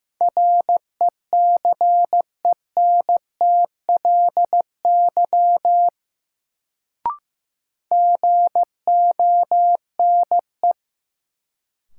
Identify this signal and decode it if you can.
Morse code, heard as a keyed tone, about 15 wpm, 700 Hz. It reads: RECENTLY GONE
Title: Sentences from Top 1200 Words - New Word Drill - 15wpm